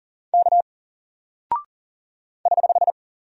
Morse code, heard as a keyed tone, about 40 wpm, 700 Hz. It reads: K <HH>
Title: Single Character (Letter:Number:Punctuation:Prosign) 40wpm